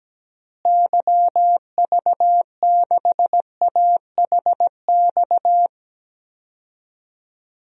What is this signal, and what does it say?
Morse code, heard as a keyed tone, about 17 wpm, 700 Hz. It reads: YV6AHX
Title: DXCC-100 Call Signs - 6 Characters - Repeated 3x - 17wpm